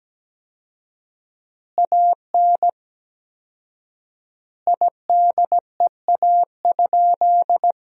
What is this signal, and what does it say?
Morse code, heard as a keyed tone, about 17 wpm, 700 Hz. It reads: AN IDEA?
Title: 4x Word-Spacing - Sentences from Top 400 Words - 17wpm